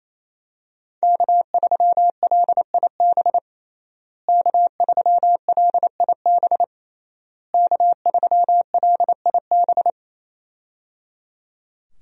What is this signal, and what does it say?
Morse code, heard as a keyed tone, about 27 wpm, 700 Hz. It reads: K3LIB K3LIB K3LIB
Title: USA Call Signs - 5 Characters - Repeated 3x - 28wpm